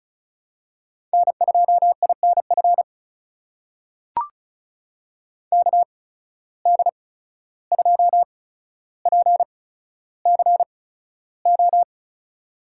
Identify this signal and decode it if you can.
Morse code, heard as a keyed tone, about 36 wpm, spaced as for 14 wpm, 700 Hz. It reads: N2INF KD2PCO
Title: USA Call Signs - 8x Character Spacing - 35wpm